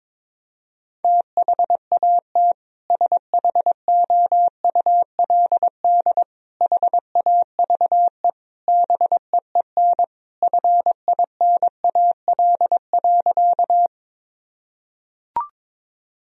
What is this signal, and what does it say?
Morse code, heard as a keyed tone, about 22 wpm, 700 Hz. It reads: THAT SHOULD HAVE BEEN FINAL.
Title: Sentences from Top 1000 Words - Review Part 2 - 22wpm